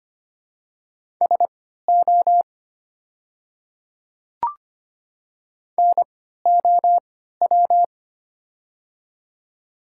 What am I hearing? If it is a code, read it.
Morse code, heard as a keyed tone, about 25 wpm, spaced as for 14 wpm, 700 Hz. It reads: SO NOW
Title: 3x Character Spacing - Top 100 Words - 25wpm